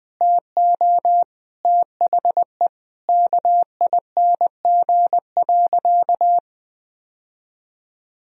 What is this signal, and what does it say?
Morse code, heard as a keyed tone, about 21 wpm, 700 Hz. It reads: TO THE KING.
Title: Sentences from Top 1000 Words - Review Part 4 - 20wpm